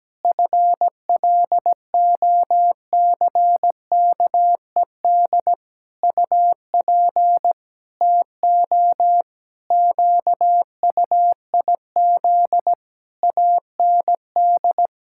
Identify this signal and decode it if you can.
Morse code, heard as a keyed tone, about 17 wpm, 700 Hz. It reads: FLOCKED UP TO QUIZ AND
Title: Pangram Challenge v2 - 17wpm